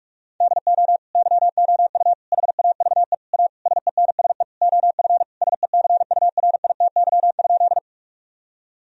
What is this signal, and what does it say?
Morse code, heard as a keyed tone, about 46 wpm, 700 Hz. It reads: DO YOU HAVE A SENSE OF SECURITY?
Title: Sentences from Top 1000 Words - Review Part 4 - 45wpm